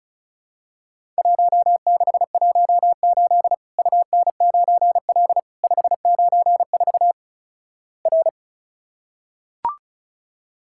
Morse code, heard as a keyed tone, about 36 wpm, 700 Hz. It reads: 1618 UN9L 594 R
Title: Worked All Europe DX Contest - QTC Exchange - 35wpm